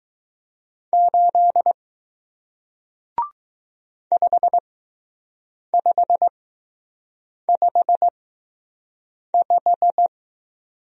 Morse code, heard as a keyed tone, about 20 wpm, spaced as for 6 wpm, 700 Hz. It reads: 8 5555